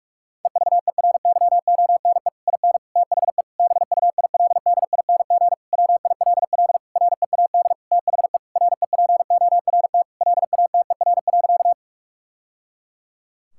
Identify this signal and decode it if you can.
Morse code, heard as a keyed tone, about 43 wpm, 700 Hz. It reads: EVERYONE IN THE BUILDING WILL READ THE REPORT LATER.
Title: Sentences from Top 1000 Words - Review Part 6 - 45wpm